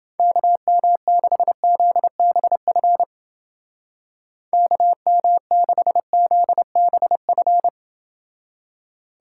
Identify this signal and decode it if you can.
Morse code, heard as a keyed tone, about 29 wpm, 700 Hz. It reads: KM6ZBF KM6ZBF